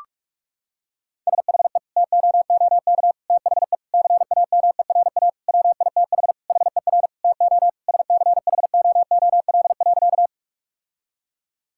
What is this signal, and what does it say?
Morse code, heard as a keyed tone, about 46 wpm, 690 Hz. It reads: SHE TOOK THE CAMERA WITH HER TO SCHOOL.
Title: Sentences from Top 1000 Words - Review Part 5 - 45wpm